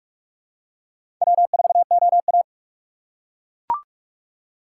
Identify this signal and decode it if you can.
Morse code, heard as a keyed tone, about 41 wpm, 700 Hz. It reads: W4OA
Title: USA Call Signs - 4 Characters - 4x Character Spacing - 45wpm